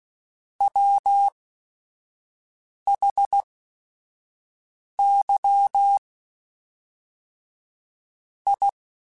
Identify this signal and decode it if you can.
Morse code, heard as a keyed tone, about 16 wpm, spaced as for 5 wpm, 800 Hz. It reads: WHY I
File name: Technical Subjects / Technical Subjects-mp3s / OR Gate @5